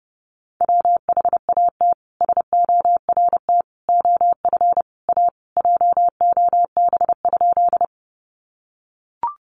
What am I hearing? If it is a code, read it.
Morse code, heard as a keyed tone, about 30 wpm, 700 Hz. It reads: WHAT SORT OF A JOB?
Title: Sentences from Top 1300 Words 30wpm